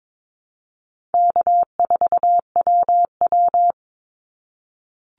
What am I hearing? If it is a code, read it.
Morse code, heard as a keyed tone, about 22 wpm, 700 Hz. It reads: K4WW
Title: Call Signs - 22wpm